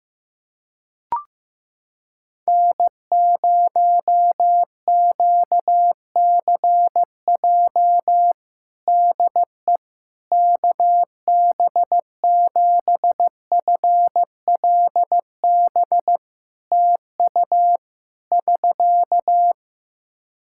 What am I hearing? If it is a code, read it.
Morse code, heard as a keyed tone, about 15 wpm, 700 Hz. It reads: N0QCJ DE KB7FLB TU <SK>
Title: Practice QSO - Exchange 4 - Ending - 15wpm